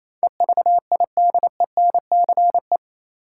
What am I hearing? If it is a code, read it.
Morse code, heard as a keyed tone, about 28 wpm, 700 Hz. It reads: EVIDENCE
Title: Sentences from Top 1300 Words - New Word Drill - 28wpm